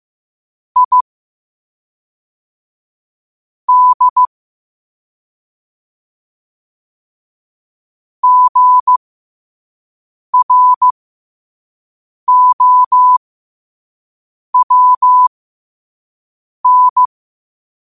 Morse code, heard as a keyed tone, about 15 wpm, spaced as for 5 wpm, 1000 Hz. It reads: I D GROWN
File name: qotdfiles / 5wpm